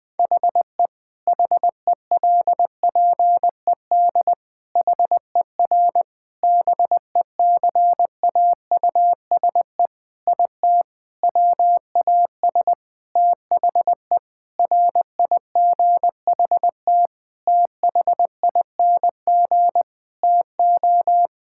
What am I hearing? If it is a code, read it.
Morse code, heard as a keyed tone, about 21 wpm, 700 Hz. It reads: HE HELPED HER BECAUSE IT WAS THE RIGHT THING TO
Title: Sentences with 8-Word Phrases - Spoken Beginning - 20wpm